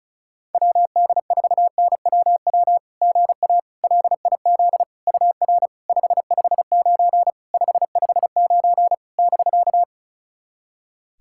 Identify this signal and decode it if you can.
Morse code, heard as a keyed tone, about 36 wpm, 690 Hz. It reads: WD4NWW GA LIZ UR 559 559 <BK>
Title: SOTA QSO Practice 35wpm